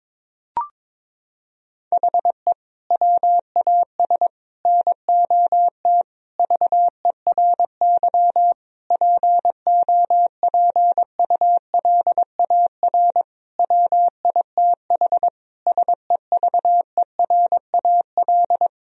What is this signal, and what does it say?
Morse code, heard as a keyed tone, about 21 wpm, 700 Hz. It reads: HE WAS NOT VERY POPULAR WITH SEVERAL
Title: Sentences from Top 1000 Words - Review Part 3 - 22wpm